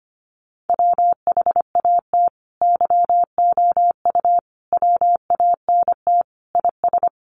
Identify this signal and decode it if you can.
Morse code, heard as a keyed tone, about 25 wpm, 700 Hz. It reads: WHAT YOU WANT IS
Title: Sentences from Top 700 Words 25wpm